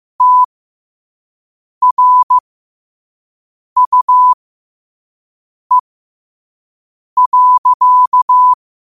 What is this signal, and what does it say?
Morse code, heard as a keyed tone, about 15 wpm, spaced as for 5 wpm, 1000 Hz. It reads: TRUE.
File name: qotdfiles / 5wpm